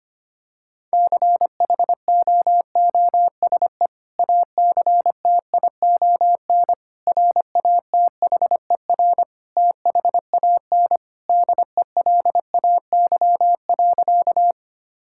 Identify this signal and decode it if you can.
Morse code, heard as a keyed tone, about 25 wpm, 700 Hz. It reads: CHOOSE ACTION RATHER THAN DELAY.
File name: practice / 25wpm / Sentences with 2-Word Phrases - 25wpm